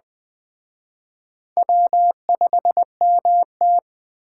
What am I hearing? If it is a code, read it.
Morse code, heard as a keyed tone, about 20 wpm, 700 Hz. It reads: W5MT
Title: Speed Racing - USA Call Signs 17wpm